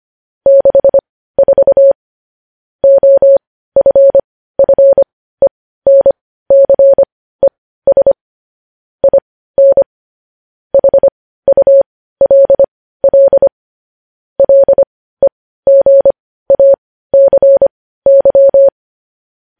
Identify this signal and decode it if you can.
Morse code, heard as a keyed tone, about 25 wpm, spaced as for 15 wpm, 550 Hz. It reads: 64 OFFENCES IN HULL LEGACY
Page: news headlines in morse code at 15 wpm